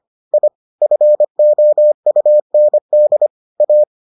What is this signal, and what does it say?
Morse code, heard as a keyed tone, about 25 wpm, 600 Hz. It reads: I FOUND A